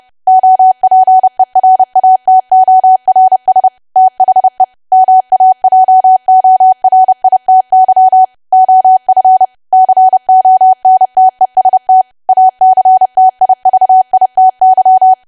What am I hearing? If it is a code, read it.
Morse code, heard as a keyed tone, about 30 wpm, 750 Hz. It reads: OPERATORS THE MAJORITY OF CONTEST ACTIVITY
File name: cwa / W1AW-Level3 / 30WPM